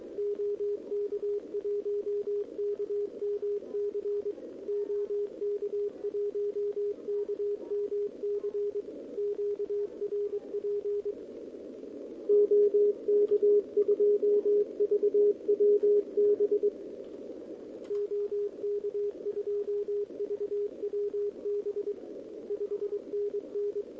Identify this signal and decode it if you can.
Morse code, heard as a keyed tone, about 22 wpm, 410 Hz. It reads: OK1KMC OK1KMC QRP OK2VWB OK2VWB 5NN
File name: OK QRP 2026 / OKQRP2026